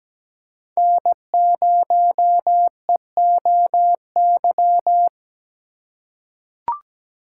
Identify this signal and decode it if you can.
Morse code, heard as a keyed tone, about 17 wpm, 700 Hz. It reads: N0EOY
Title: USA Call Signs - 5 Characters - 3x Character Spacing - 17wpm